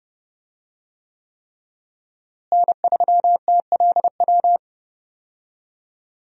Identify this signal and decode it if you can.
Morse code, heard as a keyed tone, about 30 wpm, 700 Hz. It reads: N3TLW
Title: USA Call Signs - 5 Characters - Repeated 3x - 30wpm